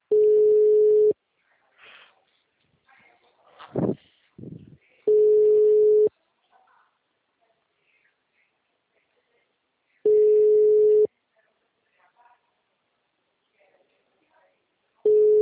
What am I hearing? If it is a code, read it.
Ringback tone (Europe, 425 Hz).
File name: out- -2013- - - .4